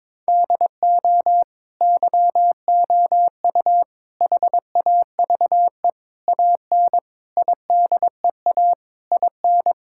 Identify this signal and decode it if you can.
Morse code, heard as a keyed tone, about 22 wpm, 700 Hz. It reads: DO YOU HAVE AN IDEA IN
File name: Sets of 9 Words - 22wpm